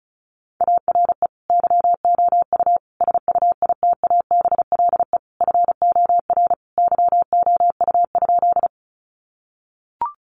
Morse code, heard as a keyed tone, about 34 wpm, 700 Hz. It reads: ARE YOU SUITABLE FOR YOU?
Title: Sentences from Top 1400 Words 35wpm